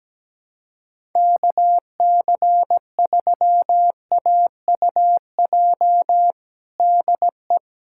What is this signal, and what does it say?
Morse code, heard as a keyed tone, about 17 wpm, 700 Hz. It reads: KC3AUJ DE
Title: USA Callsign DE Callsign - 6 Characters - 17wpm